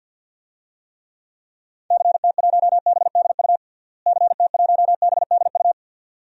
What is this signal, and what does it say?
Morse code, heard as a keyed tone, about 49 wpm, 700 Hz. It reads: CT1BDU CT1BDU
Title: DXCC-100 Call Signs - 6 Characters - Repeated 2x - 50wpm